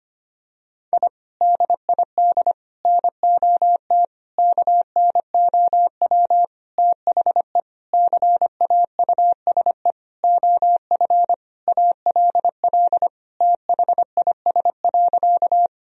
Morse code, heard as a keyed tone, about 25 wpm, 700 Hz. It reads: I DID NOT KNOW THE CAUSE OF ALL THIS.
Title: Sentences from Top 1000 Words - Review Part 1 - 25wpm